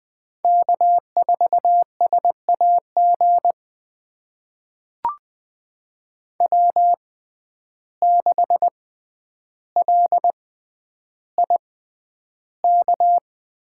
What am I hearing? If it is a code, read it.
Morse code, heard as a keyed tone, about 20 wpm, spaced as for 10 wpm, 700 Hz. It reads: K4SAG W6LIK
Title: USA Call Signs - 5 Characters - 6x Character Spacing - 20wpm